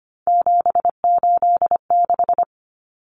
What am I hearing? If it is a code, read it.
Morse code, heard as a keyed tone, about 25 wpm, 700 Hz. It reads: 786
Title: Common Cut Numbers - 3-Digit 25wpm